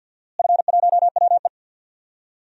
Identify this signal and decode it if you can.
Morse code, heard as a keyed tone, about 48 wpm, 700 Hz. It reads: F1WE
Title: DXCC-100 Call Signs - 4 Characters - Repeated 2x - 50wpm